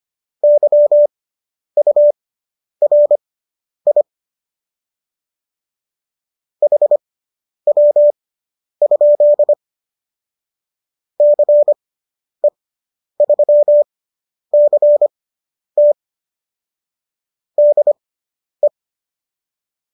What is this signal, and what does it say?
Morse code, heard as a keyed tone, about 25 wpm, spaced as for 10 wpm, 590 Hz. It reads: YURI HW? CE3CT DE